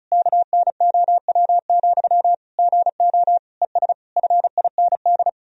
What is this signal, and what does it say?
Morse code, heard as a keyed tone, about 36 wpm, 700 Hz. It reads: KNOW, GO ES FIND
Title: Sentences from Top 200 Words and Abbreviations 35wpm